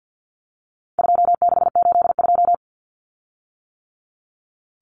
Morse code, heard as a keyed tone, about 50 wpm, 700 Hz. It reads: 2682
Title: Common Cut Numbers - 4-Digit 50wpm